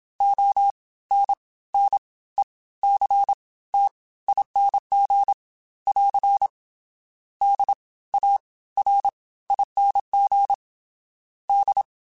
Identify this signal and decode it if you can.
Morse code, heard as a keyed tone, about 27 wpm, spaced as for 17 wpm, 800 Hz. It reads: ONNECTING <AR> DARING D
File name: Special INGs / Special INGs-mp3s / Fast ing @25